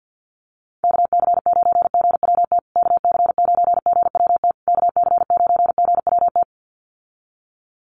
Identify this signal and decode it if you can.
Morse code, heard as a keyed tone, about 50 wpm, 700 Hz. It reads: KC9GWT KC9GWT KC9GWT
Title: USA Call Signs Repeated 3x 50wpm